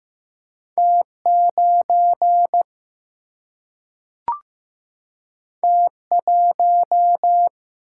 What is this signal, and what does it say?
Morse code, heard as a keyed tone, about 15 wpm, 700 Hz. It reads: T9 T1
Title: Single-Letter Single-Number 15wpm